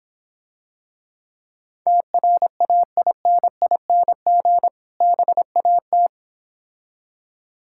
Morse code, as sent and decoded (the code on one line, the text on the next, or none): - .-. .- .. -. .. -. --. / -... .- -
TRAINING BAT